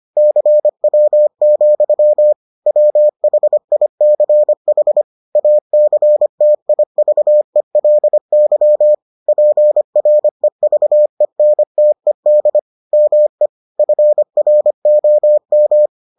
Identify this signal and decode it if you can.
Morse code, heard as a keyed tone, about 25 wpm, 600 Hz. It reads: CW, WHICH ACTIVELY PREVENTED ME FROM